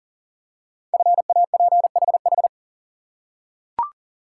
Morse code, heard as a keyed tone, about 42 wpm, 700 Hz. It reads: FAPHH
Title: Code Groups - 5 Letters - 40wpm